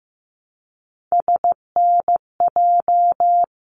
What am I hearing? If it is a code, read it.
Morse code, heard as a keyed tone, about 15 wpm, 700 Hz. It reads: SNJ
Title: ARRL Sections 15wpm